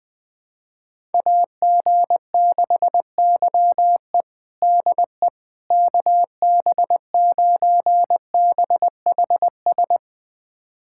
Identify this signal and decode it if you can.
Morse code, heard as a keyed tone, about 20 wpm, 700 Hz. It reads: AG6YE DE KB9BHS
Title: USA Callsign DE Callsign 20wpm